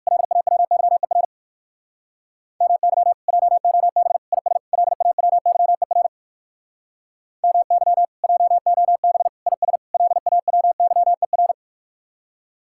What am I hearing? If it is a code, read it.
Morse code, heard as a keyed tone, about 57 wpm, 700 Hz. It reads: LAWYER MY JOB IS LAWYER MY JOB IS LAWYER